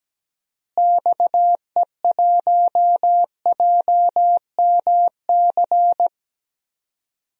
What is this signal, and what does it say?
Morse code, heard as a keyed tone, about 17 wpm, 700 Hz. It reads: XE1JMC
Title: DXCC-100 Call Signs - 6 Characters - Repeated 2x - 17wpm